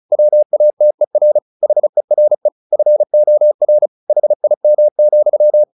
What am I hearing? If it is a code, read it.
Morse code, heard as a keyed tone, about 35 wpm, 600 Hz. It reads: WATER HERE FOR HIM,
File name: the old man and the sea / the old man and the sea 1 35wpm